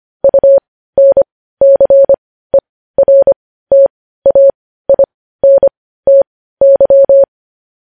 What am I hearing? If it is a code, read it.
Morse code, heard as a keyed tone, about 25 wpm, spaced as for 15 wpm, 550 Hz. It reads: UNCERTAINTY
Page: news headlines in morse code at 15 wpm